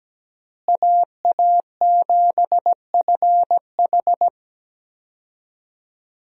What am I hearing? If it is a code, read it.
Morse code, heard as a keyed tone, about 17 wpm, 700 Hz. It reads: AA7FH
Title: USA Call Signs - 5 Characters - Repeated 3x - 17wpm